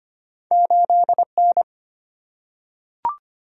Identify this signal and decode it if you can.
Morse code, heard as a keyed tone, about 25 wpm, 690 Hz. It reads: 8N
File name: Two Letter-Number Combinations - 25wpm